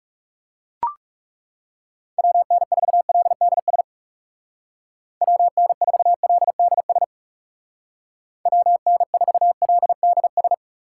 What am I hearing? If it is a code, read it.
Morse code, heard as a keyed tone, about 37 wpm, 700 Hz. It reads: WN4LDS WN4LDS WN4LDS
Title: Speed Racing - USA Call Signs 30wpm